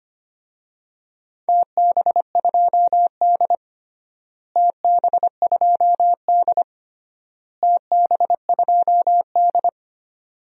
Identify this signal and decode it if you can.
Morse code, heard as a keyed tone, about 25 wpm, 700 Hz. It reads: TB2D TB2D TB2D
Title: DXCC-100 Call Signs - 4 Characters - Repeated 3x - 25wpm